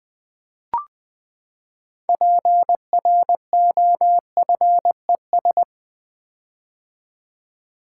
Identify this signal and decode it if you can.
Morse code, heard as a keyed tone, about 20 wpm, 700 Hz. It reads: PROFES
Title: Build A Word - 6 Letters At A Time - 12-Letter Words - 20wpm